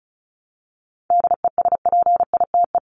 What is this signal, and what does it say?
Morse code, heard as a keyed tone, about 34 wpm, 700 Hz. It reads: DESPITE
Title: Top 751-1000 Words 35wpm